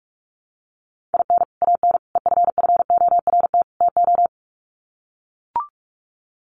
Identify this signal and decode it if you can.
Morse code, heard as a keyed tone, about 44 wpm, 700 Hz. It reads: IN AN EFFORT TO
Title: Sets of 4 Words 45wpm